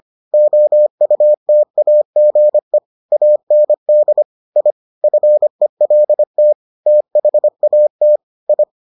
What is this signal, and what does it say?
Morse code, heard as a keyed tone, about 25 wpm, 600 Hz. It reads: OUTAGE AND I FELT THAT I